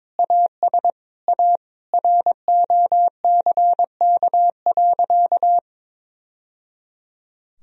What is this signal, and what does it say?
Morse code, heard as a keyed tone, about 22 wpm, 700 Hz. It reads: AS A ROCK.
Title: Sentences from Top 1000 Words - Review Part 2 - 22wpm